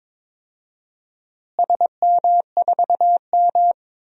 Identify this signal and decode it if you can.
Morse code, heard as a keyed tone, about 22 wpm, 700 Hz. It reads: SM4M
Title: DXCC-100 Call Signs - 4 Characters - Repeated 3x - 22wpm